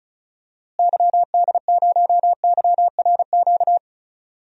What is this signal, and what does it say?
Morse code, heard as a keyed tone, about 36 wpm, 700 Hz. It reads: YD0YRQ
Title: DXCC-100 Call Signs - 6 Characters - Repeated 3x - 35wpm